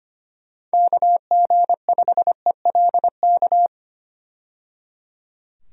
Morse code, heard as a keyed tone, about 25 wpm, 700 Hz. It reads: KG5ELK
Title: USA Call Signs - 6 Characters - 25wpm